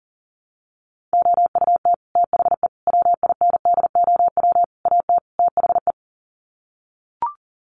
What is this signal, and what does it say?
Morse code, heard as a keyed tone, about 40 wpm, 700 Hz. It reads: OUT THE WINDOW AT THE
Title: Sets of 5 Words Encore2 40wpm